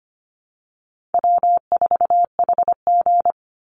Morse code, heard as a keyed tone, about 25 wpm, 700 Hz. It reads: W4HG
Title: Call Signs Repeated 2x - 25wpm